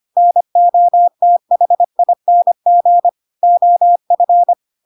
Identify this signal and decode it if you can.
Morse code, heard as a keyed tone, about 25 wpm, 700 Hz. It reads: NOTHING OF
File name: Pinocchio / Pinocchio ch7 25 25 ik1hge